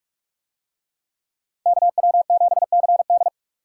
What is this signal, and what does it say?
Morse code, heard as a keyed tone, about 44 wpm, 700 Hz. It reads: KW7CD